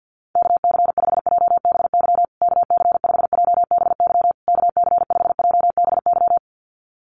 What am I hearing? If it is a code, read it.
Morse code, heard as a keyed tone, about 50 wpm, 700 Hz. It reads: KC5JBY KC5JBY KC5JBY
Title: USA Call Signs Repeated 3x 50wpm